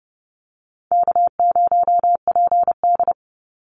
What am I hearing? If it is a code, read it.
Morse code, heard as a keyed tone, about 30 wpm, 700 Hz. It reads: K0PD
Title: Easy Call Signs - 30wpm